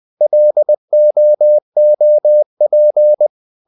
Morse code, heard as a keyed tone, about 20 wpm, 600 Hz. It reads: LOOP
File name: the old man and the sea / the old man and the sea part 1 20wpm